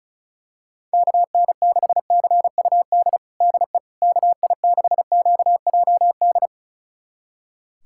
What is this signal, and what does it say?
Morse code, heard as a keyed tone, about 34 wpm, 700 Hz. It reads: KN6CUD DE KI6QJD
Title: USA Callsign DE Callsign - 6 Characters - 35wpm